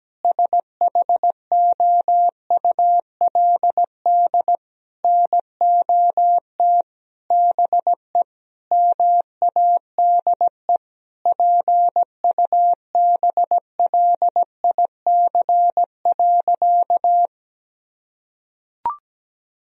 Morse code, heard as a keyed tone, about 17 wpm, 700 Hz. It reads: SHOULD NOT BE MADE PUBLIC.
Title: Sentences from Top 1000 Words - Review Part 3 - 17wpm